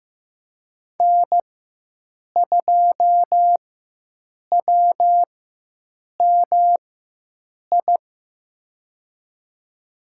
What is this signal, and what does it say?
Morse code, heard as a keyed tone, about 15 wpm, spaced as for 7 wpm, 700 Hz. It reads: N2WMI
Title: USA Call Signs - 4x Character Spacing - 15wpm